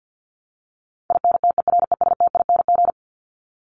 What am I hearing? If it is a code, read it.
Morse code, heard as a keyed tone, about 49 wpm, 700 Hz. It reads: INTERESTING
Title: ING Suffix - Top 100 Words with Base Words 50wpm